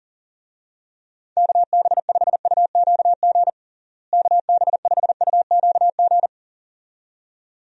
Morse code, heard as a keyed tone, about 37 wpm, 700 Hz. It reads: KB5UQG KB5UQG
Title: USA Call Signs - 6 Characters - Repeated 2x - 40wpm